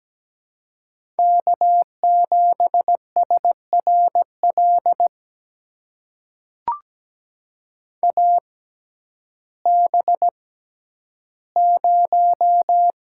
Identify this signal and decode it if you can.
Morse code, heard as a keyed tone, about 17 wpm, spaced as for 10 wpm, 700 Hz. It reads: K7SRL AB0
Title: USA Call Signs - 5 Characters - 6x Character Spacing - 17wpm